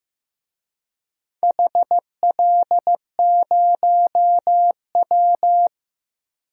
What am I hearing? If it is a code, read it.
Morse code, heard as a keyed tone, about 15 wpm, 700 Hz. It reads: HL0W
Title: DXCC-100 Call Signs - 4 Characters - 15wpm